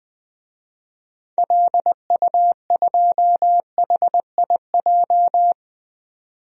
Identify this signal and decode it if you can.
Morse code, heard as a keyed tone, about 20 wpm, 700 Hz. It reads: LU2HIJ
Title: DXCC-100 Call Signs - 6 Characters - 20wpm